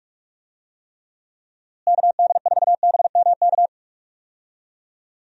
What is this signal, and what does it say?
Morse code, heard as a keyed tone, about 47 wpm, 700 Hz. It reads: KD4BMK
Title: USA Call Signs - 6 Characters - Repeated 3x - 45wpm